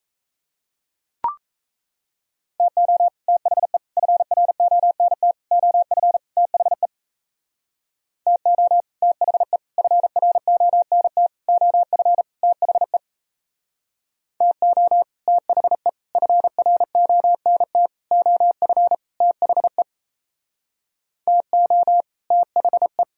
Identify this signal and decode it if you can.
Morse code, heard as a keyed tone, about 33 wpm, 700 Hz. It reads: TO THE FRONT OF THE TO THE FRONT OF THE TO THE FRONT OF THE TO THE
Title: Speed-Racing - Sets of 5 Words 28wpm